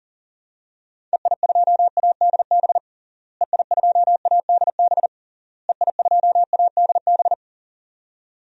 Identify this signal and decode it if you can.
Morse code, heard as a keyed tone, about 40 wpm, 700 Hz. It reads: EI2ADB EI2ADB EI2ADB
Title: DXCC-100 Call Signs - 6 Characters - Repeated 3x - 40wpm